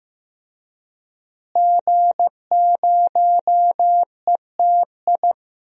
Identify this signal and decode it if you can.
Morse code, heard as a keyed tone, about 15 wpm, 700 Hz. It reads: G0ETI
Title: DXCC-100 Call Signs - 5 Characters - Repeated 3x - 15wpm